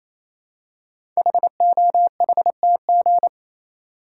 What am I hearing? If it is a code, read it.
Morse code, heard as a keyed tone, about 28 wpm, 700 Hz. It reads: HOHTG
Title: Code Groups - 5 Letters - 28wpm